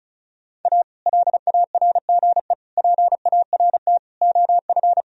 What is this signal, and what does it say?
Morse code, heard as a keyed tone, about 36 wpm, 700 Hz. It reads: A LARGE PART OF
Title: Sentences from Top 1000 Words - Review Part 1 - 35wpm